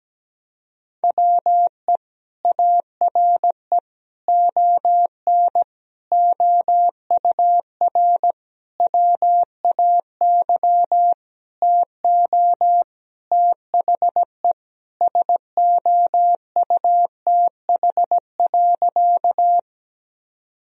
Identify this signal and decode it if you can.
Morse code, heard as a keyed tone, about 17 wpm, 700 Hz. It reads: WE ARE ON OUR WAY TO THE SOUTH.
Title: Sentences from Top 1000 Words - Review Part 2 - 17wpm